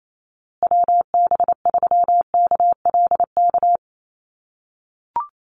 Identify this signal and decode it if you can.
Morse code, heard as a keyed tone, about 28 wpm, 700 Hz. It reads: WB3KLK
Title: USA Call Signs Repeated 3x 28wpm